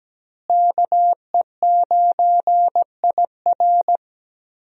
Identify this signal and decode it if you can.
Morse code, heard as a keyed tone, about 17 wpm, 700 Hz. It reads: KE9IR
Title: USA Call Signs - 5 Characters - Repeated 2x - 17wpm